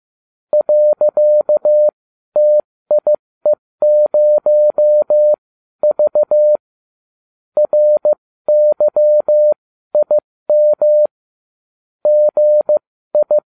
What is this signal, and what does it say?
Morse code, heard as a keyed tone, about 15 wpm, spaced as for 12 wpm, 600 Hz. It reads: .TIE0V RYIM GI